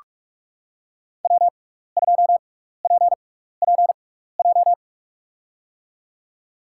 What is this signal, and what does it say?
Morse code, heard as a keyed tone, about 47 wpm, spaced as for 16 wpm, 700 Hz. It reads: W2PPJ
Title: USA Call Signs - 5 Characters - 6x Character Spacing - 45wpm